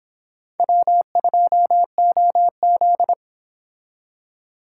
Morse code, heard as a keyed tone, about 25 wpm, 700 Hz. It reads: W2OZ